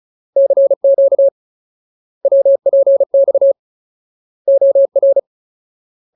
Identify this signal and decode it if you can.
Morse code, heard as a keyed tone, about 35 wpm, 550 Hz. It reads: CQ WPX OR
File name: Ultimate Word List / 35WPM Ultimate